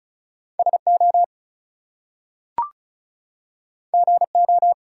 Morse code, heard as a keyed tone, about 36 wpm, 700 Hz. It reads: SO GO
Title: Two Letter Words - 35wpm